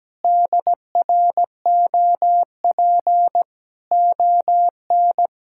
Morse code, heard as a keyed tone, about 17 wpm, 700 Hz. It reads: DROP ON
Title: Pangram Challenge v2 - 17wpm